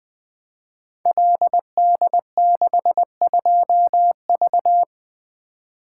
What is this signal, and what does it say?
Morse code, heard as a keyed tone, about 20 wpm, 700 Hz. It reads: LD62V
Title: Code Groups - 5 Letter:Numbers - 20wpm